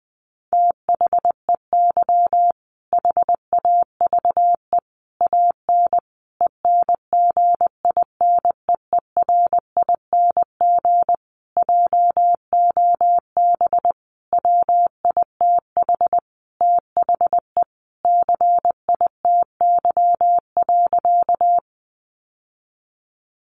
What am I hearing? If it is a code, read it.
Morse code, heard as a keyed tone, about 20 wpm, 700 Hz. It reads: THEY HAVE AN ENGINEERING JOB WITH THE CITY.
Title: Sentences from Top 1500 Words 20wpm